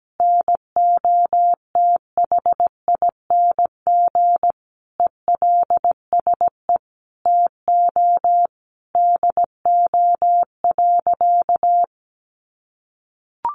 Morse code, heard as a keyed tone, about 17 wpm, 700 Hz. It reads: NOTHING ELSE TO DO.
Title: Sentences from Top 800 Words 17wpm